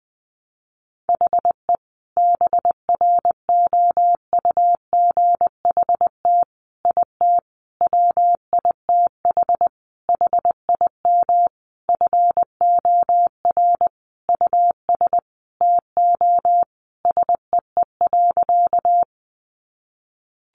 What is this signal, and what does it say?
Morse code, heard as a keyed tone, about 20 wpm, 700 Hz. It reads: HE BROUGHT IT WITH HIM FOR US TO SEE.
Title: Sentences from Top 500 Words 20wpm